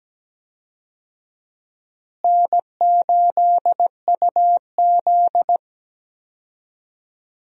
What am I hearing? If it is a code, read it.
Morse code, heard as a keyed tone, about 17 wpm, 700 Hz. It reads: N8UZ
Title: USA Call Signs - 4 Characters - Repeated 3x - 17wpm